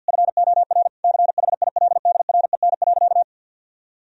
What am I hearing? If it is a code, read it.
Morse code, heard as a keyed tone, about 50 wpm, 700 Hz. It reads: FOR CHILDREN.
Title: Sentences from Top 1000 Words - Review Part 4 - 50wpm